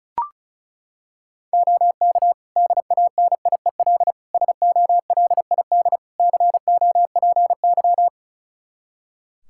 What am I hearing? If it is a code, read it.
Morse code, heard as a keyed tone, about 36 wpm, 700 Hz. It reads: OK DANIEL SOLID COPY
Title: K1USN SST QSO Practice 35wpm